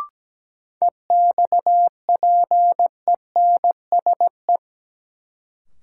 Morse code, heard as a keyed tone, about 17 wpm, 700 Hz. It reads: EXPENSE
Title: Sets of 4 Words - New Word Drill - 17wpm